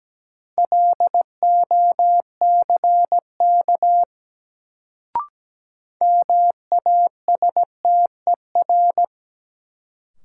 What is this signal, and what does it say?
Morse code, heard as a keyed tone, about 17 wpm, 700 Hz. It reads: LOCK MASTER
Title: Sentences from Top 1300 Words - New Word Drill - 17wpm